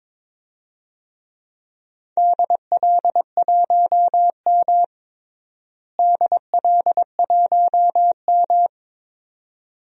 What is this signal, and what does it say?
Morse code, heard as a keyed tone, about 22 wpm, 700 Hz. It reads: DL1M DL1M
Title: DXCC-100 Call Signs - 4 Characters - Repeated 2x - 22wpm